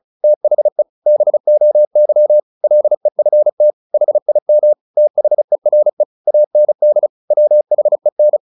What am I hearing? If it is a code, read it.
Morse code, heard as a keyed tone, about 34 wpm, 600 Hz. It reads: THE BOY LEFT HIM THERE AND WHEN
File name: the old man and the sea / the old man and the sea 1 35wpm